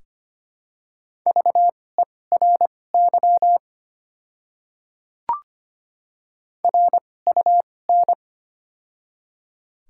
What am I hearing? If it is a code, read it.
Morse code, heard as a keyed tone, about 25 wpm, spaced as for 18 wpm, 700 Hz. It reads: VERY RUN
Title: 2x Character Spacing - Top 300 Words - 25wpm